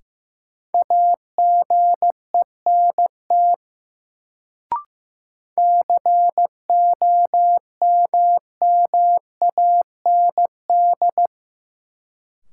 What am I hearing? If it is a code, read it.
Morse code, heard as a keyed tone, about 15 wpm, 700 Hz. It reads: AGENT COMMAND
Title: Sentences from Top 1600 Words - New Word Drill - 15wpm